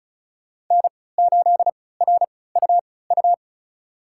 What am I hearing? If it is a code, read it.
Morse code, heard as a keyed tone, about 34 wpm, spaced as for 20 wpm, 700 Hz. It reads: N8RUU
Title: USA Call Signs - 5 Characters - 3x Character Spacing - 35wpm